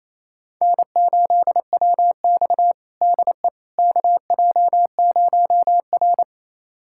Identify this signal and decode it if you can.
Morse code, heard as a keyed tone, about 28 wpm, 700 Hz. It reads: N8WX DE KJ0R
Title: USA Callsign DE Callsign - 4 Characters - 28wpm